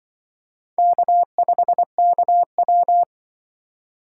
Morse code, heard as a keyed tone, about 24 wpm, 700 Hz. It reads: K5KW
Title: Speed Racing - USA 1x2 Call Signs 20wpm